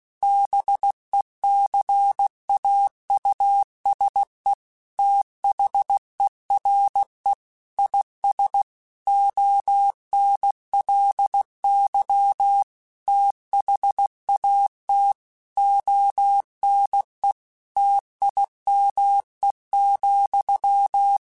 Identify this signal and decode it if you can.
Morse code, heard as a keyed tone, about 16 wpm, 800 Hz. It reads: BECAUSE THERE IS ONLY THAT ONE TIME,
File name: Technical Subjects / Technical Subjects-mp3s / AND Gate @15